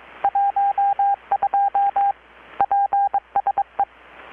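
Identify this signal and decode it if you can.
Morse code, heard as a keyed tone, about 22 wpm, 800 Hz, with radio noise. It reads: 12 PSE
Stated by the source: Morse from KPH's Night of Nights XX maritime radiotelegraphy event, received in Chiapas, Mexico about 3,800 km away on a Kenwood R-600 with a grounded 10 m long-wire antenna.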